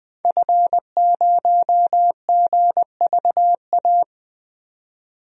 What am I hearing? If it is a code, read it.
Morse code, heard as a keyed tone, about 20 wpm, 700 Hz. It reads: F0GVA
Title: DXCC-100 Call Signs - 5 Characters - 20wpm